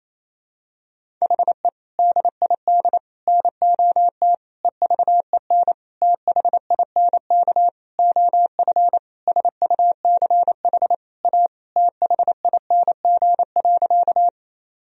Morse code, heard as a keyed tone, about 28 wpm, 700 Hz. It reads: HE DID NOT EVEN THINK OF SUCH A THING.
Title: Sentences from Top 1000 Words - Review Part 1 - 28wpm